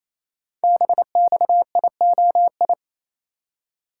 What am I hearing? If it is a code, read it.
Morse code, heard as a keyed tone, about 27 wpm, 700 Hz. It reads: BXIOI
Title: Code Groups - 5 Letters - 28wpm